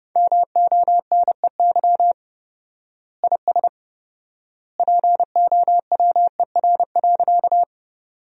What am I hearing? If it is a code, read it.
Morse code, heard as a keyed tone, about 30 wpm, 700 Hz. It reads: MONEY IS POWER.
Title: 4x Word-Spacing - Sentences from Top 500 Words - 30wpm